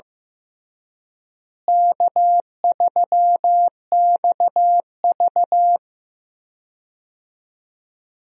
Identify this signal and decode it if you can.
Morse code, heard as a keyed tone, about 15 wpm, 700 Hz. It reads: K3XV
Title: USA Call Signs - 4 Characters - Repeated 2x - 15wpm